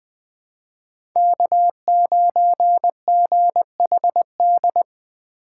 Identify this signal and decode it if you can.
Morse code, heard as a keyed tone, about 20 wpm, 700 Hz. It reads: K9GHD
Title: USA Call Signs - 5 Characters - 2x Character Spacing - 20wpm